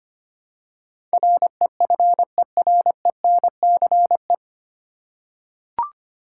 Morse code, heard as a keyed tone, about 25 wpm, 700 Hz. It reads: REFERENCE
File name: Nine Letter Words - 25wpm